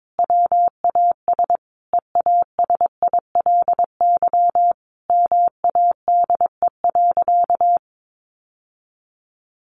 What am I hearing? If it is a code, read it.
Morse code, heard as a keyed tone, about 22 wpm, 700 Hz. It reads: WAS EASILY MADE.
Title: Sentences from Top 1200 Words 22wpm